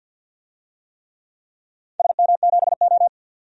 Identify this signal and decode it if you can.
Morse code, heard as a keyed tone, about 51 wpm, 700 Hz. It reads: SM7O
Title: DXCC-100 Call Signs - 4 Characters - 50wpm